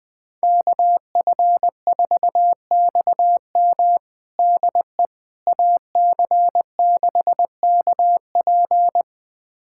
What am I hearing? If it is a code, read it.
Morse code, heard as a keyed tone, about 20 wpm, 700 Hz. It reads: KF4XM DE AC6KP
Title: USA Callsign DE Callsign - 5 Characters - 20wpm